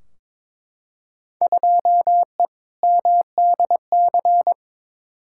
Morse code, heard as a keyed tone, about 21 wpm, 700 Hz. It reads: 2E MDC
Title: Field Day Exchanges - 22wpm